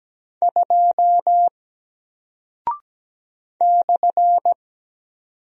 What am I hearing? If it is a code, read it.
Morse code, heard as a keyed tone, about 17 wpm, 700 Hz. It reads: 2 /
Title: Single Character (Letter:Number:Punctuation:Prosign) Alt 17wpm